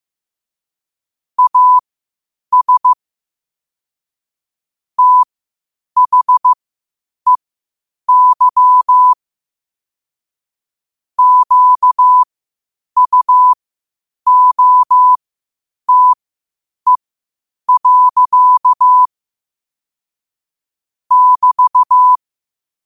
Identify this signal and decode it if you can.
Morse code, heard as a keyed tone, about 15 wpm, spaced as for 8 wpm, 1000 Hz. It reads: AS THEY QUOTE. <BT>
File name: qotdfiles / 8wpm